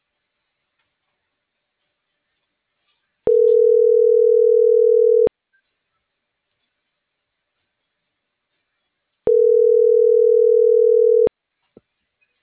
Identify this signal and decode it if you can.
Ringback tone (North America), 440 + 480 Hz.